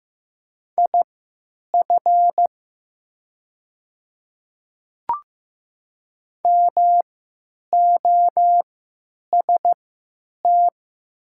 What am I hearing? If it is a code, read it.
Morse code, heard as a keyed tone, about 15 wpm, spaced as for 9 wpm, 700 Hz. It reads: IF MOST
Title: 3x Character Spacing - Top 100 Words - 15wpm